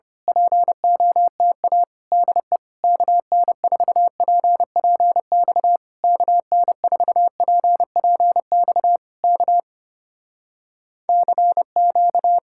Morse code, heard as a keyed tone, about 26 wpm, 700 Hz. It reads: POTA DE KN4PPX KN4PPX K CQ
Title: Speed-Racing Common QSO Phrases Encore - 25wpm